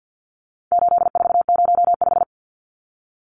Morse code, heard as a keyed tone, about 50 wpm, 700 Hz. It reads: 8405
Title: Numbers - 4-Digit 50wpm